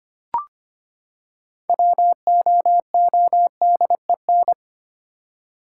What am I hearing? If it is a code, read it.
Morse code, heard as a keyed tone, about 25 wpm, 700 Hz. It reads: WOODEN
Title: Six Letter Words - 25wpm